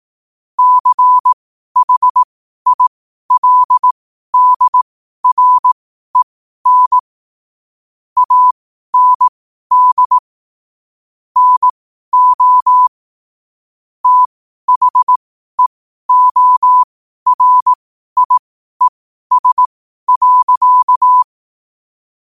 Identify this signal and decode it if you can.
Morse code, heard as a keyed tone, about 18 wpm, spaced as for 12 wpm, 1000 Hz. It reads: CHILDREN AND NO THEORIES.
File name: 12wpm